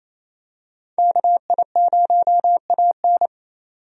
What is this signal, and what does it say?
Morse code, heard as a keyed tone, about 28 wpm, 700 Hz. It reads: KI0AN
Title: USA Call Signs - 5 Characters - 28wpm